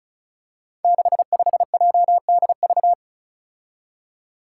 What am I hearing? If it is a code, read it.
Morse code, heard as a keyed tone, about 36 wpm, 700 Hz. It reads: 65JDV
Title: Five Challenging Letter-Number Combinations 35wpm